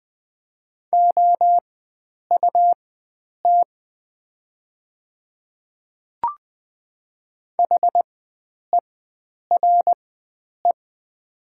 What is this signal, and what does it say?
Morse code, heard as a keyed tone, about 20 wpm, spaced as for 9 wpm, 700 Hz. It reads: OUT HERE
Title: 4x Character Spacing - Top 300 Words - 20wpm